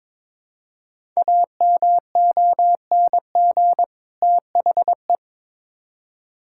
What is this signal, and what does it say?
Morse code, heard as a keyed tone, about 22 wpm, 700 Hz. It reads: AMONG THE
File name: Build A Phrase - 2-Word Phrases - 22wpm